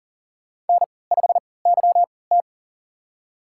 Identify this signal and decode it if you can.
Morse code, heard as a keyed tone, about 40 wpm, spaced as for 23 wpm, 700 Hz. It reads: N5YT
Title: USA Call Signs - 3x Character Spacing - 40wpm